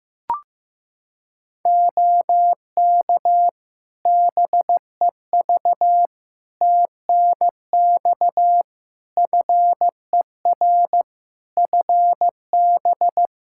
Morse code, heard as a keyed tone, about 15 wpm, 700 Hz. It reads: OK BEV TNX FER FB
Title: Practice QSO - Exchange 4 - Ending - 15wpm